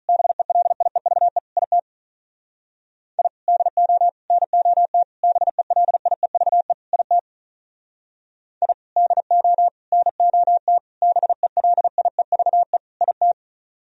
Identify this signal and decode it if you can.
Morse code, heard as a keyed tone, about 45 wpm, 700 Hz. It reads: BELIEVE IT I DO NOT BELIEVE IT I DO NOT BELIEVE IT